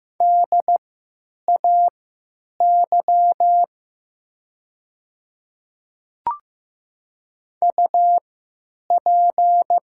Morse code, heard as a keyed tone, about 15 wpm, spaced as for 9 wpm, 700 Hz. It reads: DAY UP
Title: 3x Character Spacing - Top 100 Words - 15wpm